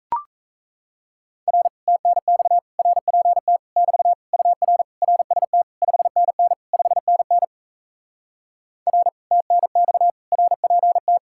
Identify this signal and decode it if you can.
Morse code, heard as a keyed tone, about 37 wpm, 700 Hz. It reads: R TNX RPT <BT> UR RST 5NN 5NN R TNX RPT